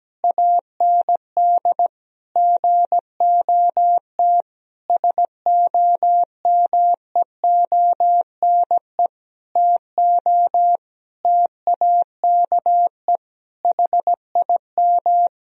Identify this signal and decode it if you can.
Morse code, heard as a keyed tone, about 17 wpm, 700 Hz. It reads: AND GOT SOMEONE TO TAKE HIM
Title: Sentences from Top 1000 Words - Review Part 4 - 17wpm